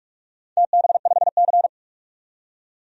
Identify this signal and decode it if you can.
Morse code, heard as a keyed tone, about 44 wpm, 700 Hz. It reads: TB5C
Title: DXCC-100 Call Signs - 4 Characters - 45wpm